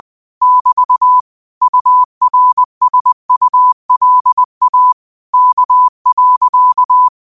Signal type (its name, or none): Morse code